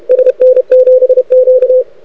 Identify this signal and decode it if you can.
Morse code, heard as a keyed tone, about 30 wpm, 510 Hz. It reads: SN7Q
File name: rec contest / arrl cw 2018 / test 32